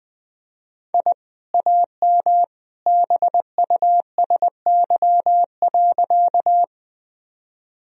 Morse code, heard as a keyed tone, about 20 wpm, 700 Hz. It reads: I AM BUSY.
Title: Sentences from Top 1000 Words - Review Part 3 - 20wpm